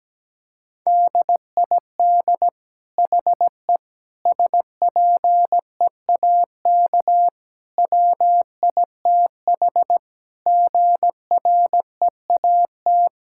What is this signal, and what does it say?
Morse code, heard as a keyed tone, about 17 wpm, 700 Hz. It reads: DID HE SPEAK WITH GREAT
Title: Sentences from Top 1000 Words - Review Part 6 - 17wpm